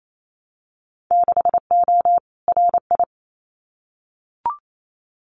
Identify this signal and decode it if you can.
Morse code, heard as a keyed tone, about 28 wpm, 690 Hz. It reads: 6O RI
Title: Winter Field Day Exchanges Encore 28wpm